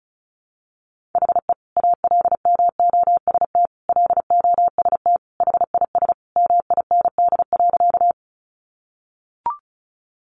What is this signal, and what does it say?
Morse code, heard as a keyed tone, about 34 wpm, 700 Hz. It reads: HE ALMOST LOST HIS MIND.
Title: Sentences from Top 1400 Words 35wpm